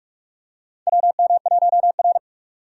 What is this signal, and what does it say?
Morse code, heard as a keyed tone, about 43 wpm, 700 Hz. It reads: WM1R
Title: USA Call Signs - 4 Characters - 4x Character Spacing - 45wpm